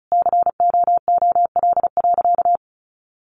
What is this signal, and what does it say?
Morse code, heard as a keyed tone, about 34 wpm, 700 Hz. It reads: COOL.
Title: Sentences from Top 1200 Words 35wpm